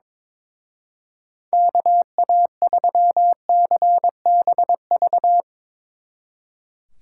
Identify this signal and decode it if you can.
Morse code, heard as a keyed tone, about 22 wpm, 700 Hz. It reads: KA3CBV
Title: Speed Racing - USA 2x3 Call Signs 22wpm